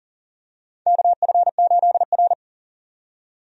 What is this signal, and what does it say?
Morse code, heard as a keyed tone, about 40 wpm, 700 Hz. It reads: KF8R
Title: USA Call Signs - 4 Characters - 40wpm